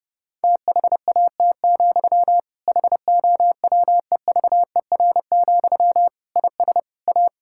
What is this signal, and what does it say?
Morse code, heard as a keyed tone, about 32 wpm, 700 Hz. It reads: THAT, HOWEVER, IS A
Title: Sentences from Top 1000 Words - Review Part 6 - 30wpm